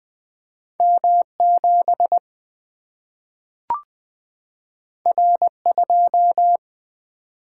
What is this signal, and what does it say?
Morse code, heard as a keyed tone, about 20 wpm, 700 Hz. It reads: M7 R2
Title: Single-Letter Single-Number 20wpm